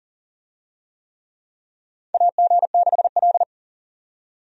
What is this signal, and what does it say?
Morse code, heard as a keyed tone, about 39 wpm, 700 Hz. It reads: AG6L
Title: USA Call Signs - 4 Characters - 40wpm